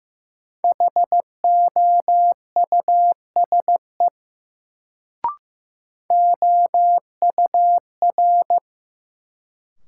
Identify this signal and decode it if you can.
Morse code, heard as a keyed tone, about 15 wpm, 700 Hz. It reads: HOUSE OUR
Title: Sentences from Top 200 Words - New Word Drill - 15wpm